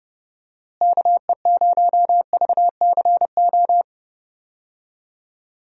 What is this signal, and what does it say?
Morse code, heard as a keyed tone, about 30 wpm, 700 Hz. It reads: KE0VCO
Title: USA Call Signs - 6 Characters - Repeated 2x - 30wpm